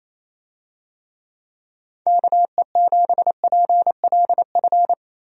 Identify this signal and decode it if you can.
Morse code, heard as a keyed tone, about 28 wpm, 700 Hz. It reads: KE7PLF
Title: USA Call Signs - 6 Characters - Repeated 3x - 28wpm